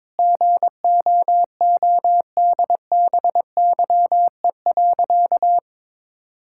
Morse code, heard as a keyed tone, about 21 wpm, 700 Hz. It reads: GOODBYE.
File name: Sentences with 7-Word Phrases - 22wpm